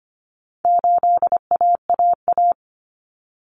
Morse code, heard as a keyed tone, about 25 wpm, 700 Hz. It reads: 8AAA
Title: Common Cut Numbers - 4-Digit 25wpm